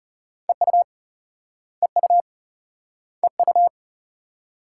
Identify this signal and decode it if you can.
Morse code, heard as a keyed tone, about 38 wpm, 700 Hz. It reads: EU EU EU